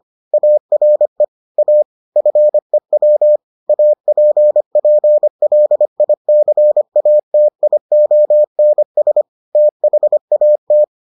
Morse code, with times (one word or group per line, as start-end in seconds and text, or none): 0.3-1.2 ARE
1.6-1.8 A
2.2-3.4 FEW
3.7-9.2 APPLICATIONS
9.6-10.8 THAT